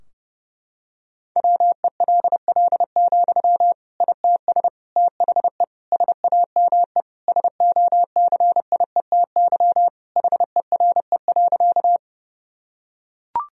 Morse code, heard as a keyed tone, about 30 wpm, 700 Hz. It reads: WELL, ITS THE SAME SOCIETY HERE.
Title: Sentences from Top 1000 Words - Review Part 4 - 30wpm